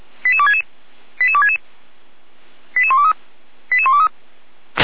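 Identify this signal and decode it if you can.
Five-tone selective call (ZVEI), heard as quick runs of five tones, tones of about 70 ms. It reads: 88266 88266 88123 88123
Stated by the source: ZVEI five-tone call 88129 received on an SDR, from feuerwehr-orlamuende.org fire brigade audio server.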